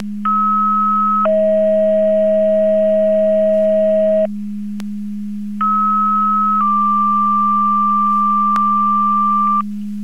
Two-tone page, heard as a one-second tone, then a three-second tone, about 1290 then 640 Hz, about 1290 then 1180 Hz, with noise. It is heard from 0.3 to 9.6 s.
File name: Portage 154.385 Paging